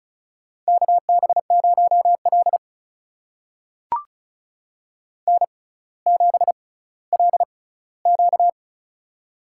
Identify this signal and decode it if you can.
Morse code, heard as a keyed tone, about 36 wpm, spaced as for 18 wpm, 700 Hz. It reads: KB0L N7LQ
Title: USA Call Signs - 4 Characters - 6x Character Spacing - 35wpm